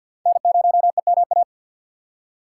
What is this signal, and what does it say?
Morse code, heard as a keyed tone, about 53 wpm, 700 Hz. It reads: N0EMA
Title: USA Call Signs - 5 Characters - 8x Character Spacing - 50wpm